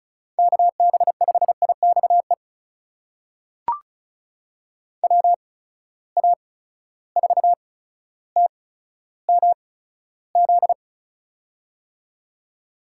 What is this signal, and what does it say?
Morse code, heard as a keyed tone, about 34 wpm, spaced as for 15 wpm, 700 Hz. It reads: KB5IXE WA4TMZ
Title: USA Call Signs - 6 Characters - 8x Character Spacing - 35wpm